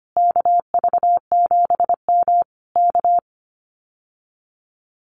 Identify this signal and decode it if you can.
Morse code, heard as a keyed tone, about 25 wpm, 700 Hz. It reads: KV7M K
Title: Common QSO Phrases Encore 25wpm